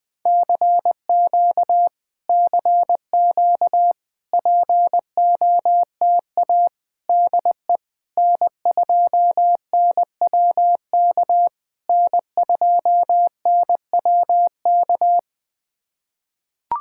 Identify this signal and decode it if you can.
Morse code, heard as a keyed tone, about 21 wpm, 700 Hz. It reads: CQ CQ POTA DE N2NWK N2NWK
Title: POTA QSO Practice - 20wpm